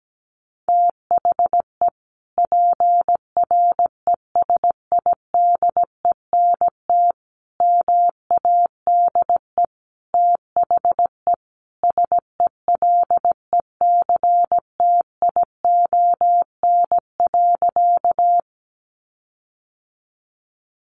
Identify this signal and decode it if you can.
Morse code, heard as a keyed tone, about 17 wpm, 700 Hz. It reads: THE PRESIDENT MADE THE SELECTION.
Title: Sentences from Top 1400 Words 17wpm